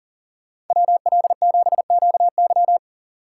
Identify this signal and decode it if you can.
Morse code, heard as a keyed tone, about 49 wpm, 700 Hz. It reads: WL7QY
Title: Speed Racing - USA 2x2 Call Signs 40wpm